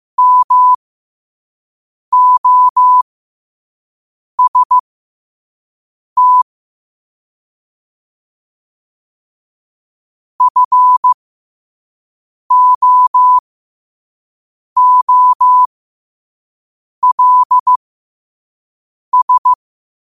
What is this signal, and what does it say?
Morse code, heard as a keyed tone, about 15 wpm, spaced as for 5 wpm, 1000 Hz. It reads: MOST FOOLS
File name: qotdfiles / 5wpm